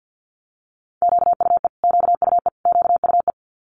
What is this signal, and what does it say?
Morse code, heard as a keyed tone, about 50 wpm, 700 Hz. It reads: QUE QUE QUE
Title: Multi-letter Phonemes Repeated 3x 50wpm